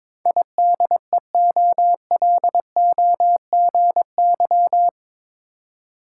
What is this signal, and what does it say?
Morse code, heard as a keyed tone, about 22 wpm, 700 Hz. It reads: IDEOLOGY
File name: practice / 22wpm / Eight Letter Words - 22wpm